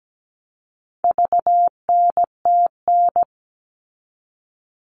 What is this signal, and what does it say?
Morse code, heard as a keyed tone, about 17 wpm, 700 Hz. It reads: VNTN
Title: All Cut Numbers - 4-Digit 17wpm